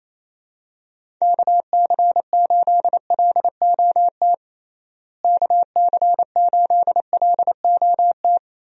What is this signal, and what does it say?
Morse code, heard as a keyed tone, about 28 wpm, 700 Hz. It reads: KC8LOT KC8LOT
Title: USA Call Signs - 6 Characters - Repeated 2x - 28wpm